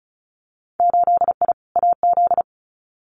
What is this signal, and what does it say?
Morse code, heard as a keyed tone, about 36 wpm, 700 Hz. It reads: 8I AZ
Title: Winter Field Day Exchanges Encore 35wpm